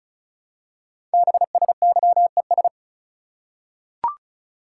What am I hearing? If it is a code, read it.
Morse code, heard as a keyed tone, about 36 wpm, 700 Hz. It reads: BSYES
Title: Code Groups - 5 Letters - 35wpm